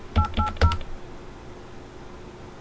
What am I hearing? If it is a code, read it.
DTMF, heard as telephone touch tones, tones of about 110 ms. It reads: 880